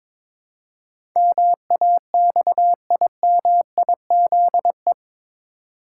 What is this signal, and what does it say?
Morse code, heard as a keyed tone, about 22 wpm, 700 Hz. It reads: MAXIMIZE
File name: practice / 22wpm / Eight Letter Words - 22wpm